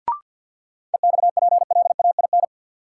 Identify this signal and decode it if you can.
Morse code, heard as a keyed tone, about 50 wpm, 700 Hz. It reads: EXPLAIN
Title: Sentences from Top 1100 Words - New Word Drill - 50wpm